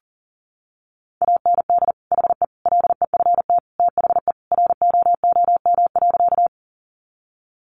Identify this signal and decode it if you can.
Morse code, heard as a keyed tone, about 40 wpm, 700 Hz. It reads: AND HE LEFT THE ROOM.
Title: Sentences from Top 300 Words 40wpm